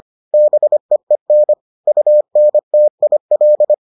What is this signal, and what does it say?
Morse code, heard as a keyed tone, about 25 wpm, 600 Hz. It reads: BEEN UNTIL